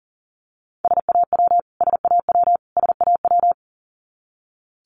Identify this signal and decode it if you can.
Morse code, heard as a keyed tone, about 40 wpm, 700 Hz. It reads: SAW SAW SAW
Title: Top 500 Words Repeated 40wpm